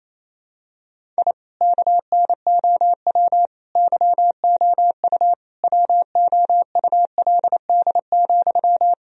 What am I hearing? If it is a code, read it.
Morse code, heard as a keyed tone, about 28 wpm, 700 Hz. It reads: I KNOW YOU WOULD,
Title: Sentences from Top 1000 Words - Review Part 1 - 28wpm